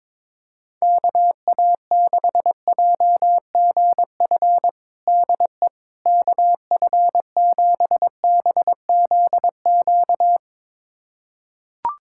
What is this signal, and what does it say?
Morse code, heard as a keyed tone, about 21 wpm, 700 Hz. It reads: KA6JGF DE KF7BZQ
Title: USA Callsign DE Callsign - 6 Characters - 22wpm